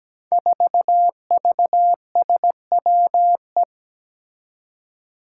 Morse code, heard as a keyed tone, about 17 wpm, 700 Hz. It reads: 4VSWE
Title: Five Challenging Letter-Number Combinations 17wpm